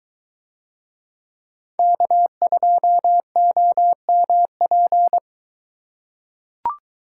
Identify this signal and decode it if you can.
Morse code, heard as a keyed tone, about 22 wpm, 700 Hz. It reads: K2OMP